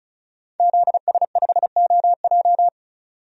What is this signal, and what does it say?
Morse code, heard as a keyed tone, about 36 wpm, 700 Hz. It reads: ZS5OJ
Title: DXCC-100 Call Signs - 5 Characters - 35wpm